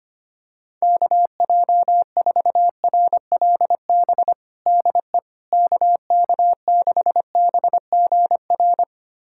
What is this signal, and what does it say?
Morse code, heard as a keyed tone, about 25 wpm, 700 Hz. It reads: KJ4RLB DE KK6BGR
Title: USA Callsign DE Callsign - 6 Characters - 25wpm